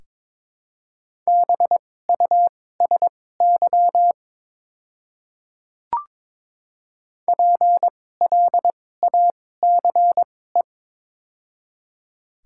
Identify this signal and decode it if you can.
Morse code, heard as a keyed tone, about 22 wpm, spaced as for 16 wpm, 700 Hz. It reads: BUSY PLACE
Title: 2x Character Spacing - Top 500 Words - 22wpm